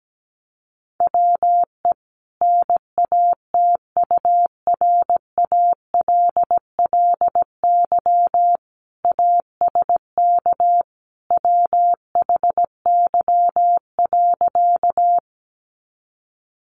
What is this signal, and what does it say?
Morse code, heard as a keyed tone, about 17 wpm, 700 Hz. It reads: WE NATURALLY ASK WHY.
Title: Sentences from Top 1300 Words 17wpm